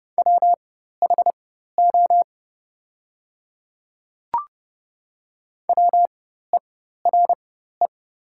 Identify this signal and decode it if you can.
Morse code, heard as a keyed tone, about 30 wpm, spaced as for 14 wpm, 700 Hz. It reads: WHO WERE
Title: 4x Character Spacing - Top 100 Words - 30wpm